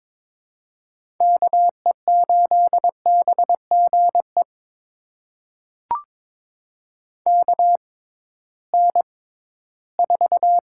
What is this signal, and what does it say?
Morse code, heard as a keyed tone, about 22 wpm, spaced as for 14 wpm, 700 Hz. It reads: KE8BGE KN4
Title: USA Call Signs - 6 Characters - 6x Character Spacing - 22wpm